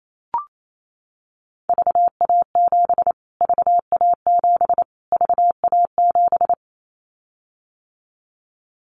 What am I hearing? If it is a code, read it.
Morse code, heard as a keyed tone, about 28 wpm, 700 Hz. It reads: VA7 VA7 VA7
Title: Callsign Prefixes for Top 25 DXCC - Repeated 3x - 28wpm